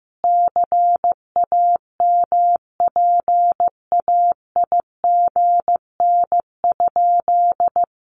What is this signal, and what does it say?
Morse code, heard as a keyed tone, about 15 wpm, 700 Hz. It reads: CAMPAIGN?
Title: Sentences from Top 1200 Words 15wpm